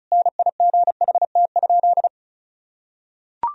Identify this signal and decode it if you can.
Morse code, heard as a keyed tone, about 36 wpm, 700 Hz. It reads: NIGHT?
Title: Sentences from Top 1000 Words - Review Part 4 - 35wpm